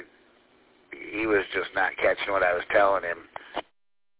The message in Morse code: none

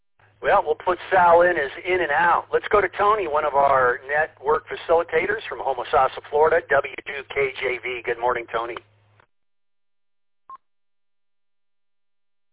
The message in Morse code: none

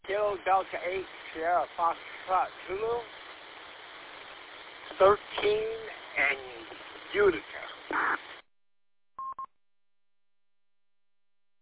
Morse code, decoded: none

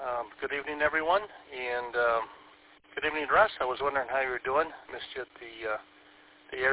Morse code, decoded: none

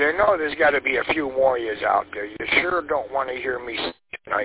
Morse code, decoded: none

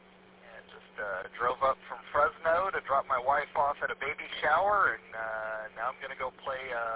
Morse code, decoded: none

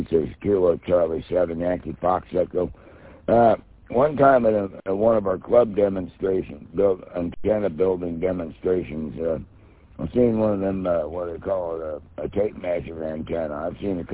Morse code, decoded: none

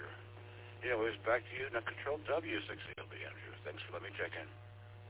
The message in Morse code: none

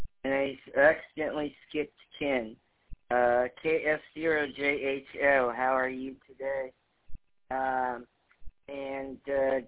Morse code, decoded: none

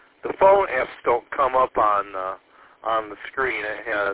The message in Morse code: none

